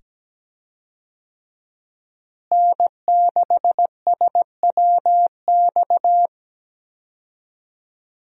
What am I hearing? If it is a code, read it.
Morse code, heard as a keyed tone, about 17 wpm, 700 Hz. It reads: N6SWX
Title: USA Call Signs - 5 Characters - Repeated 2x - 17wpm